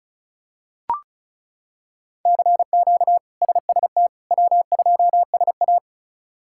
Morse code, heard as a keyed tone, about 36 wpm, 700 Hz. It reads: CQ SST W2SA
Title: K1USN SST QSO Practice - 35wpm